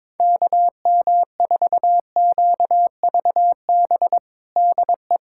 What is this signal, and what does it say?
Morse code, heard as a keyed tone, about 22 wpm, 700 Hz. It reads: KM4QVB DE
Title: USA Callsign DE Callsign - 6 Characters - 22wpm